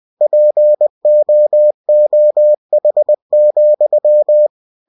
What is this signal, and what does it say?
Morse code, heard as a keyed tone, about 20 wpm, 600 Hz. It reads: POOH,
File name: cw-books / winnie-the-pooh / winnie-20wpm-page-0016